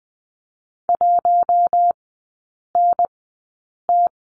Morse code, heard as a keyed tone, about 20 wpm, spaced as for 8 wpm, 700 Hz. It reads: 1NT
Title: 184 Lesson 30e - Letter b - Three Characters 20wpm